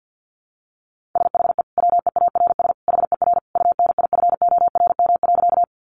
Morse code, heard as a keyed tone, about 50 wpm, 700 Hz. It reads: SHE WEARS HER UNIFORM.
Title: Easy Phrases 50wpm